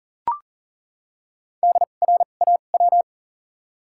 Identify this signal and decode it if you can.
Morse code, heard as a keyed tone, about 40 wpm, spaced as for 29 wpm, 700 Hz. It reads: DRAW
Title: 2x Character Spacing - Top 300 Words - 40wpm